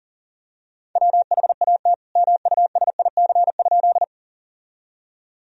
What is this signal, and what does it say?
Morse code, heard as a keyed tone, about 40 wpm, 700 Hz. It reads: WHAT MUSIC?
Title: Sentences from Top 1000 Words - Review Part 1 - 40wpm